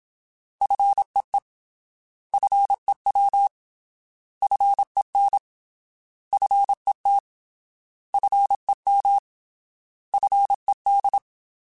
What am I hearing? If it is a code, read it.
Morse code, heard as a keyed tone, about 27 wpm, 800 Hz. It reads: FEE FEW FEN FET FEM FED
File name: Alphabetical / Alphabetical mp3s / Fs @25